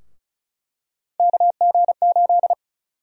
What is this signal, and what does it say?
Morse code, heard as a keyed tone, about 36 wpm, 700 Hz. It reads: KG8
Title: USA Call Sign Prefixes - Repeated 2x - 35wpm